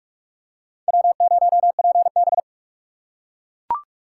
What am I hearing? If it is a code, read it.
Morse code, heard as a keyed tone, about 43 wpm, 700 Hz. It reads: W0PB
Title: USA Call Signs - 4 Characters - 8x Character Spacing - 45wpm